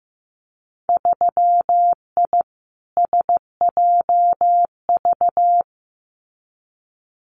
Morse code, heard as a keyed tone, about 15 wpm, 700 Hz. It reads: 3I SJV
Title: Winter Field Day Exchanges Encore 15wpm